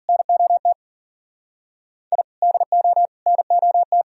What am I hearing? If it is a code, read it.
Morse code, heard as a keyed tone, about 52 wpm, 700 Hz. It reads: NOT I DO NOT